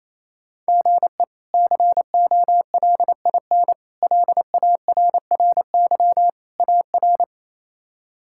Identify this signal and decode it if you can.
Morse code, heard as a keyed tone, about 28 wpm, 700 Hz. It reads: GE COLIN LARRY AR
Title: K1USN SST QSO Practice - 28wpm